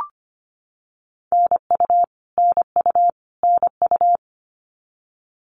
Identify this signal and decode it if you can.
Morse code, heard as a keyed tone, about 25 wpm, 700 Hz. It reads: NU NU NU
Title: Canadian Provinces and Territories - Repeated 3x - 25wpm